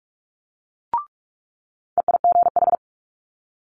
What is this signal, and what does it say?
Morse code, heard as a keyed tone, about 46 wpm, 700 Hz. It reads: EIGH
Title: Multi-letter Phonemes Quick Fire 45wpm